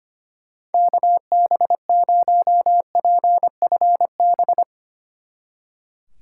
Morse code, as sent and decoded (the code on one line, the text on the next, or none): -.- -... ----- .--. ..-. -...
KB0PFB